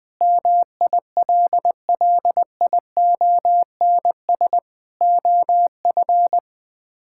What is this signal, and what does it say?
Morse code, heard as a keyed tone, about 21 wpm, 700 Hz. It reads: MILLIONS OF
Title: Build A Phrase - 5-Word Phrases - 20wpm